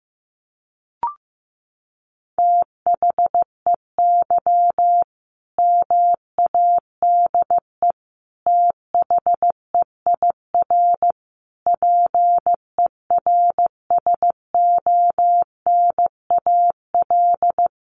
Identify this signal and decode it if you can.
Morse code, heard as a keyed tone, about 14 wpm, 700 Hz. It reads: THEY MADE THEIR PERSONAL
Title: Sentences from Top 1100 Words 15wpm